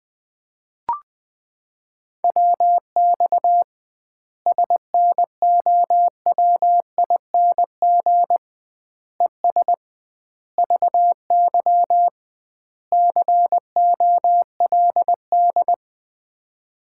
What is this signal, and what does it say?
Morse code, heard as a keyed tone, about 20 wpm, 700 Hz. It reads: WX SNOWING ES VY COLD
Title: Common QSO Phrases - 2x Word-Spacing - 20wpm